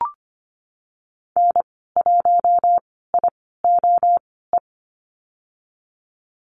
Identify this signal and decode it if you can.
Morse code, heard as a keyed tone, about 25 wpm, spaced as for 16 wpm, 700 Hz. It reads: N1IOE
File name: Lesson 15g - Number 5 - Call Signs 25wpm